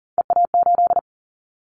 Morse code, heard as a keyed tone, about 40 wpm, 700 Hz. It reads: EA8
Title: Callsign Prefixes for Top 100 DXCC - 40wpm